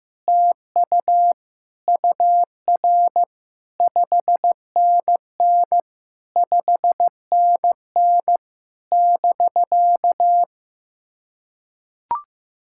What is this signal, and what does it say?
Morse code, heard as a keyed tone, about 15 wpm, 700 Hz. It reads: TU UR 5NN 5NN <BK>
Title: SOTA QSO Practice 15wpm